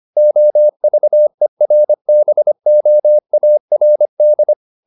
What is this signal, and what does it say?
Morse code, heard as a keyed tone, about 25 wpm, 600 Hz. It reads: OVERBOARD
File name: the old man and the sea part 1 25wpm